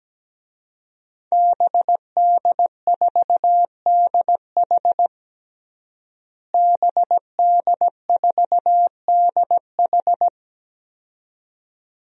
Morse code, heard as a keyed tone, about 17 wpm, 700 Hz. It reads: BD4DH BD4DH
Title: DXCC-100 Call Signs - 5 Characters - Repeated 2x - 17wpm